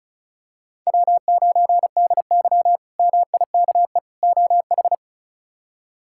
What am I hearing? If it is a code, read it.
Morse code, heard as a keyed tone, about 34 wpm, 700 Hz. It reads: W9DY MIKE OH
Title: K1USN SST QSO Practice - 35wpm